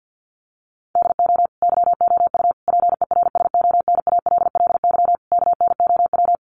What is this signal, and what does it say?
Morse code, heard as a keyed tone, about 49 wpm, 700 Hz. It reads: DO YOU PERSONALLY KNOW
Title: Sentences from Top 1400 Words 50wpm